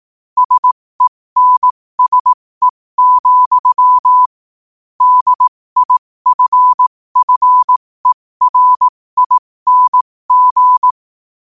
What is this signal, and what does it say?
Morse code, heard as a keyed tone, about 18 wpm, spaced as for 15 wpm, 1000 Hz. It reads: SENSE, DIFFERING
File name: qotdfiles / 15wpm